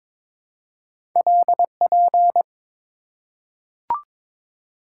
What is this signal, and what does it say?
Morse code, heard as a keyed tone, about 22 wpm, 700 Hz. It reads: LP
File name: practice / 22wpm / Two Letter-Number Combinations - 22wpm